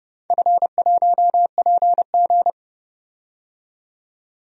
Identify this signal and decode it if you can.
Morse code, heard as a keyed tone, about 30 wpm, 700 Hz. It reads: F1PG
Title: DXCC-100 Call Signs - 4 Characters - 30wpm